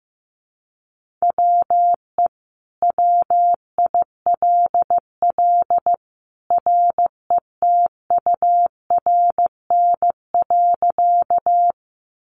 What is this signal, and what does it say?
Morse code, heard as a keyed tone, about 15 wpm, 700 Hz. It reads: WE WILL RETURN.
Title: Sentences from Top 800 Words 15wpm